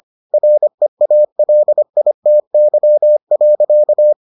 Morse code, heard as a keyed tone, about 25 wpm, 600 Hz. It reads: REALITY.